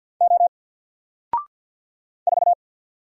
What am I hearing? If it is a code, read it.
Morse code, heard as a keyed tone, about 50 wpm, 700 Hz. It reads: O 4
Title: Single Letter-Number 50wpm